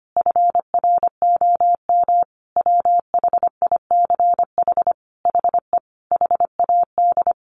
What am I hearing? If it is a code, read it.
Morse code, heard as a keyed tone, about 25 wpm, 700 Hz. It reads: FROM WHICH HE HAD
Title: Sentences from Top 1500 Words 25wpm